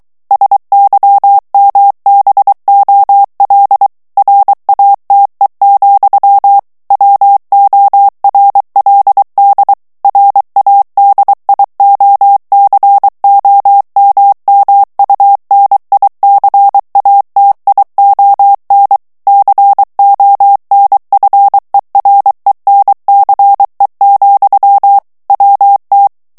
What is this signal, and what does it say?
Morse code, heard as a keyed tone, about 23 wpm, 800 Hz. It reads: SYMBOL RATE, WORLD RADIOCOMMUNICATION CONFERENCE, WT
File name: ANGLAIS texte clair 3 vitesse 22 mots minute